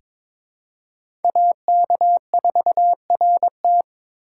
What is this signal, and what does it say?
Morse code, heard as a keyed tone, about 22 wpm, 700 Hz. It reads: AK4RT
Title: USA Call Signs - 3x Character Spacing - 22wpm